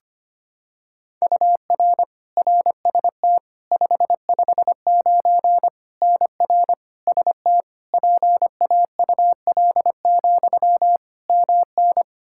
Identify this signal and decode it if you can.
Morse code, heard as a keyed tone, about 25 wpm, 700 Hz. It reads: UR RST 559 NR ST PAUL, MN
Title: Practice QSO - Exchange 1 - Beginning - 25wpm